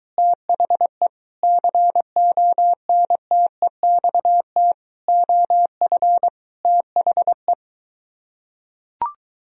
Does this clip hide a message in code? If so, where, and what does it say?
Yes: Morse code from 0.2 to 7.5 s, heard as a keyed tone, about 23 wpm, 700 Hz. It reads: THE CONTEXT OF THE